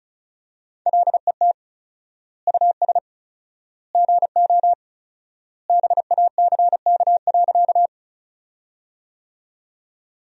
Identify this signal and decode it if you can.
Morse code, heard as a keyed tone, about 36 wpm, 700 Hz. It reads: LET US GO BACK.
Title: 4x Word-Spacing - Sentences from Top 300 Words - 35wpm